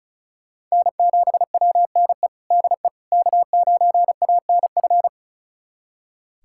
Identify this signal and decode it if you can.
Morse code, heard as a keyed tone, about 36 wpm, 700 Hz. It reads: N7WNE DE K9ANF
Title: USA Callsign DE Callsign - 5 Characters - 35wpm